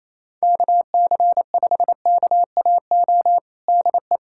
Morse code, heard as a keyed tone, about 28 wpm, 700 Hz. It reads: KC5KAO DE
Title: USA Callsign DE Callsign 28wpm